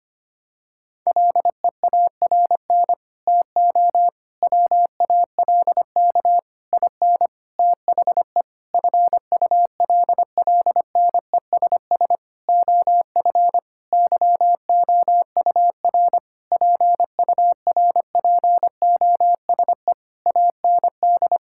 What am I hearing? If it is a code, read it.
Morse code, heard as a keyed tone, about 25 wpm, 700 Hz. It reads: LEARN TO WALK IN THE FULLNESS OF YOUR PURPOSE AND
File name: practice / 25wpm / Sentences with 9-Word Phrases - 25wpm